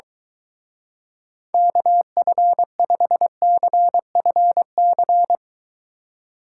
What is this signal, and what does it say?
Morse code, heard as a keyed tone, about 23 wpm, 700 Hz. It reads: KF5CFC